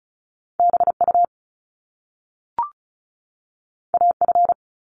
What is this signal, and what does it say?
Morse code, heard as a keyed tone, about 36 wpm, 700 Hz. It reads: BU AF
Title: Top Two Letter Combinations in Words 35wpm